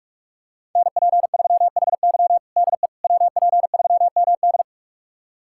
Morse code, heard as a keyed tone, about 45 wpm, 700 Hz. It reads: NP3HY DE WP3MD
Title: USA Callsign DE Callsign - 5 Characters - 45wpm